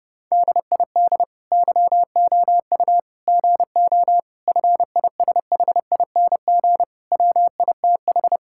Morse code, heard as a keyed tone, about 30 wpm, 700 Hz. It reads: DID YOU GO FISHING WITH
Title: Sentences from Top 1000 Words - Review Part 5 - 30wpm